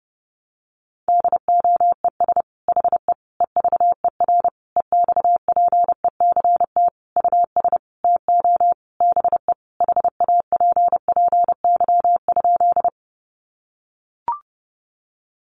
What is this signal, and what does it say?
Morse code, heard as a keyed tone, about 30 wpm, 700 Hz. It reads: DOES HE EVER EXPECT US TO BE HAPPY?
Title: Sentences from Top 1100 Words 30wpm